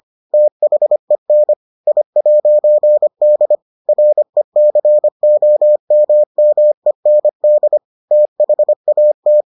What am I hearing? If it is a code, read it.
Morse code, heard as a keyed tone, about 25 wpm, 600 Hz. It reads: THEN I'D RECOMMEND THAT